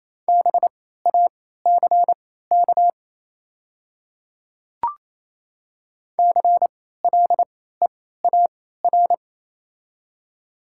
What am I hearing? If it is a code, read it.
Morse code, heard as a keyed tone, about 28 wpm, spaced as for 16 wpm, 700 Hz. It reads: BACK CLEAR
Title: 3x Character Spacing - Top 500 Words - 28wpm